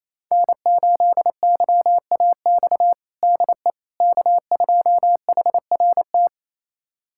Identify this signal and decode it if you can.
Morse code, heard as a keyed tone, about 28 wpm, 690 Hz. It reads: N8YAX DE K2HRT
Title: USA Callsign DE Callsign - 5 Characters - 28wpm